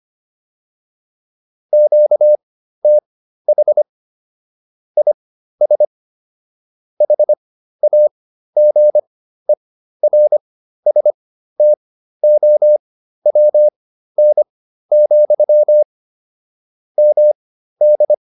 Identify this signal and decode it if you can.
Morse code, heard as a keyed tone, about 25 wpm, spaced as for 13 wpm, 600 Hz. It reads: QTH IS HAGERSTOWN, MD